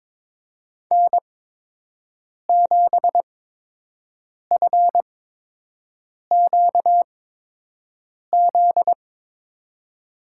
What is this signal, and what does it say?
Morse code, heard as a keyed tone, about 22 wpm, spaced as for 6 wpm, 700 Hz. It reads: N7FQZ
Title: USA Call Signs - 8x Character Spacing - 22wpm